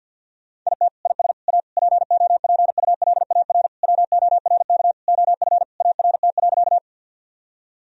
Morse code, heard as a keyed tone, about 50 wpm, 700 Hz. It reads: IT IS A POPULAR WORK OF ART.
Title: Sentences from Top 1000 Words - Review Part 3 - 50wpm